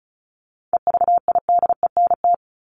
Morse code, heard as a keyed tone, about 34 wpm, 700 Hz. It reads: EVIDENT
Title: LY Suffix - Top 100 Words with Base Words 35wpm